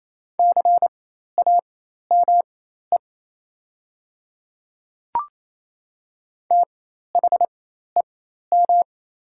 Morse code, heard as a keyed tone, about 28 wpm, spaced as for 13 wpm, 690 Hz. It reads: CAME THEM
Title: 4x Character Spacing - Top 200 Words - 28wpm